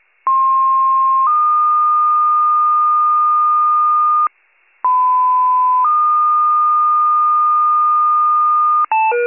Two-tone page, heard as a one-second tone, then a three-second tone, about 1060 then 1220 Hz, about 1010 then 1220 Hz. It is heard from 0.3 to 8.8 s.